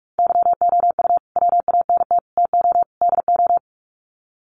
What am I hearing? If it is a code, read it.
Morse code, heard as a keyed tone, about 44 wpm, 700 Hz. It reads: YOU WANT TO DO
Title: Sets of 4 Words 45wpm